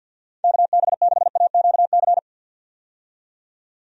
Morse code, heard as a keyed tone, about 51 wpm, 700 Hz. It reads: KB6AQC
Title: USA Call Signs - 6 Characters - 50wpm